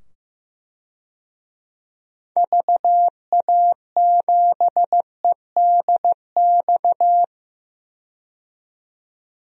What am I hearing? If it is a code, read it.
Morse code, heard as a keyed tone, about 15 wpm, 700 Hz. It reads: VA7EDX
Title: DXCC-100 Call Signs - 6 Characters - 15wpm